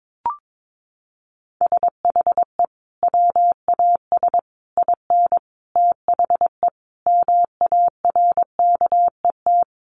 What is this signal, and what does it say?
Morse code, heard as a keyed tone, about 22 wpm, 700 Hz. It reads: SHE WAS IN THE MARKET
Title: Sentences from Top 600 Words 22wpm